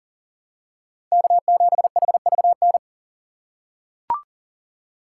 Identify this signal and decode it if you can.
Morse code, heard as a keyed tone, about 39 wpm, 700 Hz. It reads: K7HVN
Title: USA Call Signs - 3x Character Spacing - 40wpm